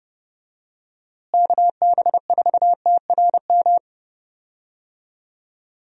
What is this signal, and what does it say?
Morse code, heard as a keyed tone, about 30 wpm, 700 Hz. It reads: KB4TRM
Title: USA Call Signs - 6 Characters - Repeated 2x - 30wpm